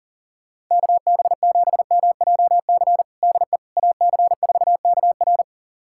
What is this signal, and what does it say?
Morse code, heard as a keyed tone, about 37 wpm, 700 Hz. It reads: KB7MJC DE AC4KR
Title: USA Callsign DE Callsign 40wpm